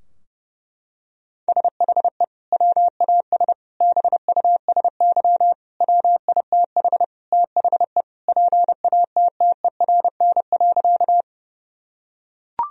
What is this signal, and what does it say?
Morse code, heard as a keyed tone, about 30 wpm, 700 Hz. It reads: SHE WAS BUSY WITH THE PATTERN.
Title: Sentences from Top 1000 Words - Review Part 2 - 30wpm